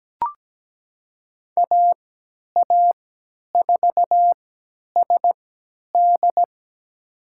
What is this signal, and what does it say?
Morse code, heard as a keyed tone, about 17 wpm, spaced as for 10 wpm, 700 Hz. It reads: AA4SD
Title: USA Call Signs - 5 Characters - 3x Character Spacing - 17wpm